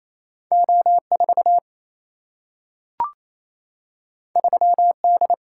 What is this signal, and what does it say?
Morse code, heard as a keyed tone, about 28 wpm, 700 Hz. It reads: O4 3D
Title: Single-Letter Single-Number 28wpm